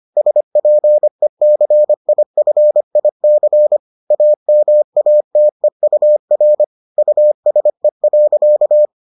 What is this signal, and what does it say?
Morse code, heard as a keyed tone, about 25 wpm, 600 Hz. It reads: SPECIFIC AMATEUR USE.